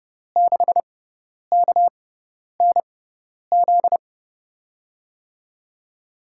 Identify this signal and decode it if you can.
Morse code, heard as a keyed tone, about 30 wpm, spaced as for 10 wpm, 700 Hz. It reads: 6KNZ
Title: USA Call Signs - 6 Characters - 6x Character Spacing - 30wpm